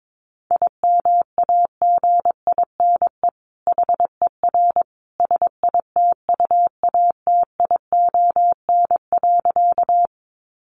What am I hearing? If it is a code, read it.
Morse code, heard as a keyed tone, about 22 wpm, 700 Hz. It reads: IMAGINE HER SITUATION.
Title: Sentences from Top 1400 Words 22wpm